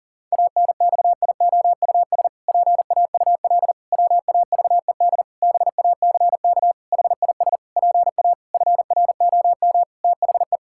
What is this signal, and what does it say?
Morse code, heard as a keyed tone, about 37 wpm, 700 Hz. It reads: ANXIOUS PAUL WAVED BACK HIS PA FROM THE
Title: Pangram Challenge v2 - 40wpm